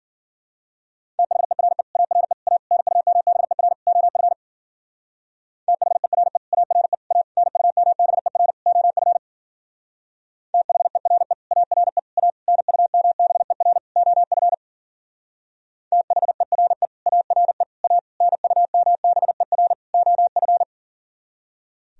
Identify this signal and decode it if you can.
Morse code, heard as a keyed tone, about 51 wpm, 700 Hz. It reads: THERE ARE A NUMBER OF THERE ARE A NUMBER OF THERE ARE A NUMBER OF THERE ARE A NUMBER OF